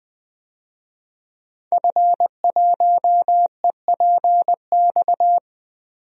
Morse code, heard as a keyed tone, about 21 wpm, 700 Hz. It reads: F1EPX
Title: DXCC-100 Call Signs - 5 Characters - Repeated 2x - 20wpm